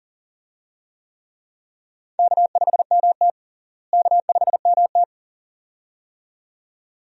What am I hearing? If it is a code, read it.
Morse code, heard as a keyed tone, about 40 wpm, 700 Hz. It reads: K5MT K5MT
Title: USA Call Signs - 4 Characters - Repeated 2x - 40wpm